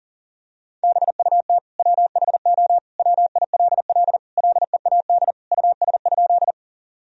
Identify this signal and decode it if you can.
Morse code, heard as a keyed tone, about 40 wpm, 700 Hz. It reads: BUT WHO WILL LEAD US?
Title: Sentences from Top 1000 Words - Review Part 2 - 40wpm